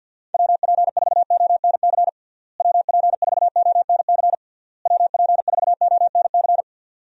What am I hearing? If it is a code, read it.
Morse code, heard as a keyed tone, about 52 wpm, 700 Hz. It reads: WP4ONC WP4ONC WP4ONC
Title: USA Call Signs - 6 Characters - Repeated 3x - 50wpm